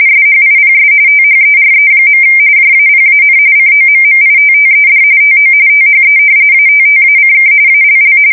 RTTY, about 45.45 baud. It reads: SHIPS THERE. JAPAN COMMITTED TO THE ATTACK ALL
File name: fsk170 / battleshipnj